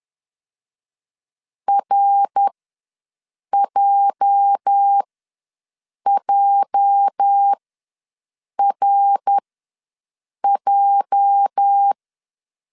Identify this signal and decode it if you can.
Morse code, heard as a keyed tone, about 11 wpm, spaced as for 6 wpm, 800 Hz. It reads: RJJRJ
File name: LEZIONI / 8 YJ45 / 8 YJ ws 10wpm 4043